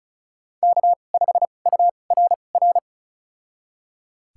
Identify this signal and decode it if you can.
Morse code, heard as a keyed tone, about 36 wpm, spaced as for 25 wpm, 700 Hz. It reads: K5URR
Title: USA Call Signs - 5 Characters - 2x Character Spacing - 35wpm